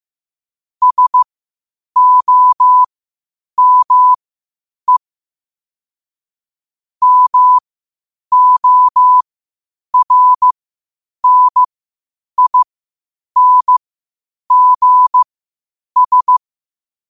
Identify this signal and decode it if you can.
Morse code, heard as a keyed tone, about 15 wpm, spaced as for 8 wpm, 1000 Hz. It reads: SOME MORNINGS